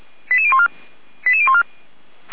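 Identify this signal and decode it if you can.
Five-tone selective call (ZVEI), heard as quick runs of five tones, tones of about 70 ms. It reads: 88014 88014
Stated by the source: Received by SDR.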